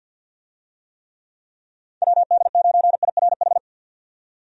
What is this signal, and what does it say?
Morse code, heard as a keyed tone, about 50 wpm, 700 Hz. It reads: WD9IRH
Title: USA Call Signs - 6 Characters - 50wpm